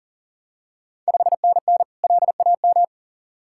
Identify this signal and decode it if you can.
Morse code, heard as a keyed tone, about 40 wpm, 690 Hz. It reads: 5NN LAM
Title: Ontario QSO Party Exchange 40wpm